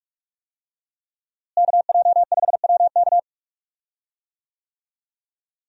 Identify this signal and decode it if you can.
Morse code, heard as a keyed tone, about 46 wpm, 700 Hz. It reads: KJ5WK
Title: USA Call Signs - 5 Characters - Repeated 2x - 45wpm